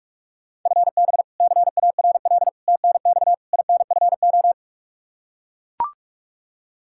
Morse code, heard as a keyed tone, about 41 wpm, 700 Hz. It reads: FB CARL TNX INFO
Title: Practice QSO - Exchange 3 - More Info - 45wpm